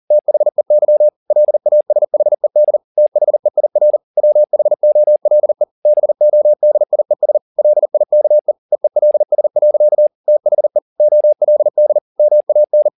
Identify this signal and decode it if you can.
Morse code, heard as a keyed tone, about 40 wpm, 600 Hz. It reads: THEY LASHED THEIR WHOLE BODIES LIKE EELS. THE OLD MAN
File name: the old man and the sea 2 40wpm